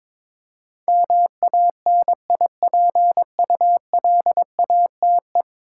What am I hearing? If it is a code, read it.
Morse code, heard as a keyed tone, about 22 wpm, 700 Hz. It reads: MANIPULATE
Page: morse-code-ninja-practice-22wpm